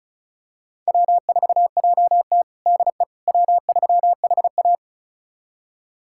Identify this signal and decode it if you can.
Morse code, heard as a keyed tone, about 36 wpm, 700 Hz. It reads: W4JT DE W3HA
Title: USA Callsign DE Callsign - 4 Characters - 35wpm